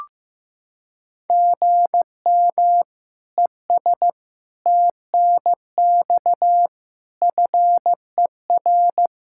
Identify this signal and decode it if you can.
Morse code, heard as a keyed tone, about 15 wpm, 700 Hz. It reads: GM ES TNX FER
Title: Practice QSO - Exchange 1 - Beginning - 15wpm